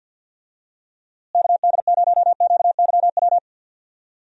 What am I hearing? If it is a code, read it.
Morse code, heard as a keyed tone, about 46 wpm, 700 Hz. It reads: KD0QYW
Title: USA Call Signs - 6 Characters - Repeated 2x - 50wpm